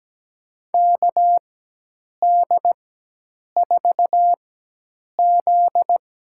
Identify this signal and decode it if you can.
Morse code, heard as a keyed tone, about 17 wpm, spaced as for 8 wpm, 700 Hz. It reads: KD4Z
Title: USA Call Signs - 4x Character Spacing - 17wpm